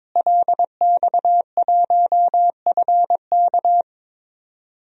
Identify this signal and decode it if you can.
Morse code, heard as a keyed tone, about 22 wpm, 700 Hz. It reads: LX1FK
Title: DXCC-100 Call Signs - 5 Characters - Repeated 2x - 22wpm